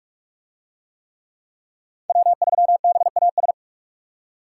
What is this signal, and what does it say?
Morse code, heard as a keyed tone, about 43 wpm, 700 Hz. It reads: W3BAS